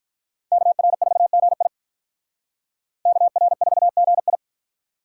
Morse code, heard as a keyed tone, about 46 wpm, 700 Hz. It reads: KR4GI KR4GI